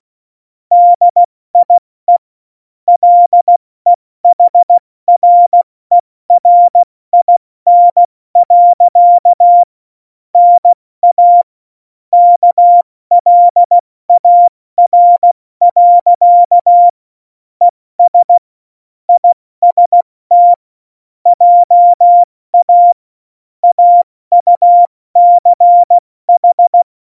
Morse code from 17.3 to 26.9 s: ES IST JA AUCH